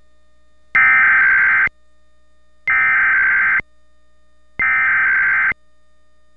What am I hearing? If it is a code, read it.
Emergency Alert System (SAME) tones (header bursts), heard as harsh buzzing bursts. It is heard from 0.8 to 5.5 s.
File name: WHYY